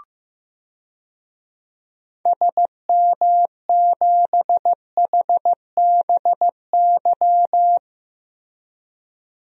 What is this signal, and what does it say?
Morse code, heard as a keyed tone, about 15 wpm, 700 Hz. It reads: SM7HBY
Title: DXCC-100 Call Signs - 6 Characters - Repeated 3x - 15wpm